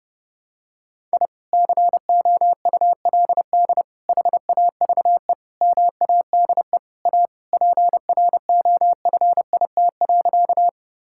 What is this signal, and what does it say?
Morse code, heard as a keyed tone, about 30 wpm, 700 Hz. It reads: I COULD HAVE MADE A PROFIT.
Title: Sentences from Top 1000 Words - Review Part 5 - 30wpm